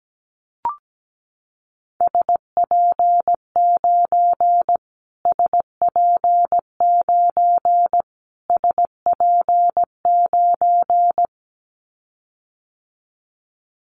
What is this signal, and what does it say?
Morse code, heard as a keyed tone, about 17 wpm, 700 Hz. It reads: SP9 SP9 SP9
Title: Callsign Prefixes for Top 25 DXCC - Repeated 3x - 17wpm